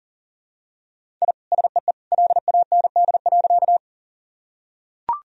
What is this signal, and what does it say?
Morse code, heard as a keyed tone, about 40 wpm, 700 Hz. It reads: I SEE LAND.
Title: Sentences from Top 1000 Words - Review Part 1 - 40wpm